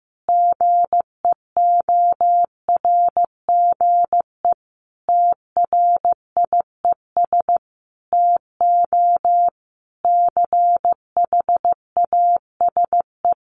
Morse code, heard as a keyed tone, about 15 wpm, 700 Hz. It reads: GEORGE TRIES TO CHASE
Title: Additional Easy Phrases 15wpm